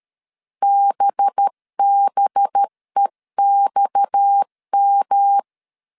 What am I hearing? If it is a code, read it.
Morse code, heard as a keyed tone, about 13 wpm, 800 Hz. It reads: BBEXM
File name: CW / DET / 1 6 1500C S C 12wpm 3533